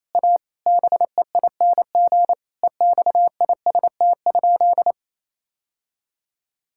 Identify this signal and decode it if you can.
Morse code, heard as a keyed tone, about 27 wpm, 700 Hz. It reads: A BEING EXIST?
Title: Sentences from Top 1000 Words - Review Part 5 - 28wpm